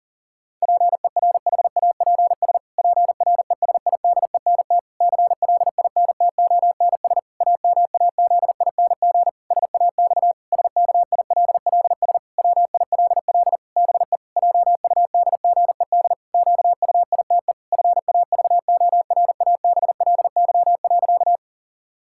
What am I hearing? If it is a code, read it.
Morse code, heard as a keyed tone, about 40 wpm, 700 Hz. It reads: PERHAPS PRESIDENT CLINTONS AMAZING SAX SKILLS WILL BE JUDGED QUITE FAVORABLY.
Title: Pangram Challenge v2 - 40wpm